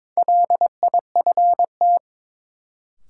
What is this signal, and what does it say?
Morse code, heard as a keyed tone, about 22 wpm, 700 Hz. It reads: LIFT
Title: Sentences from Top 1000 Words - New Word Drill - 22wpm